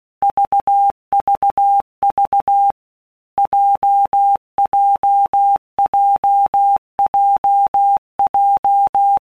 Morse code, heard as a keyed tone, about 16 wpm, 800 Hz. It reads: VVV JJJJJ
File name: Basic Letter Groups By Blocks / Groups@15 / JZ17@15